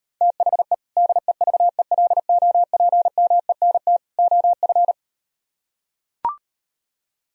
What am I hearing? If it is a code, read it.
Morse code, heard as a keyed tone, about 38 wpm, 700 Hz. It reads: THE DEVELOPMENT OF